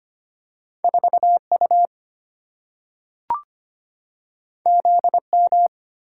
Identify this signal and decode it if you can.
Morse code, heard as a keyed tone, about 25 wpm, 700 Hz. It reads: 4U ZM
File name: Two Letter-Number Combinations - 25wpm